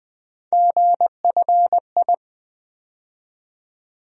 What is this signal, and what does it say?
Morse code, heard as a keyed tone, about 20 wpm, 700 Hz. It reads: GFI
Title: USA Call Sign Postfixes - 3 Letters 20wpm